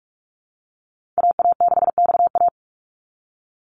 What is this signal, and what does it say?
Morse code, heard as a keyed tone, about 44 wpm, 700 Hz. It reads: AA6XA
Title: NA SOTA Activator Callsigns 45wpm